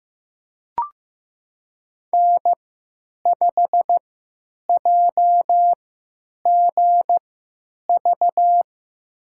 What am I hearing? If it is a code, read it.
Morse code, heard as a keyed tone, about 15 wpm, spaced as for 9 wpm, 700 Hz. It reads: N5JGV
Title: USA Call Signs - 3x Character Spacing - 15wpm